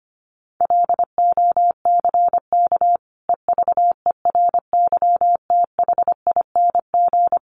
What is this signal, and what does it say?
Morse code, heard as a keyed tone, about 25 wpm, 700 Hz. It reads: LOCK EVERYTHING
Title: Sentences from Top 1300 Words 25wpm